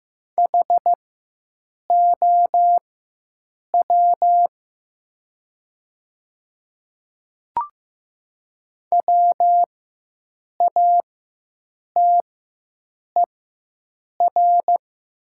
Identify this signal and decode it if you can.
Morse code, heard as a keyed tone, about 15 wpm, spaced as for 7 wpm, 700 Hz. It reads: HOW WATER
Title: 4x Character Spacing - Top 100 Words - 15wpm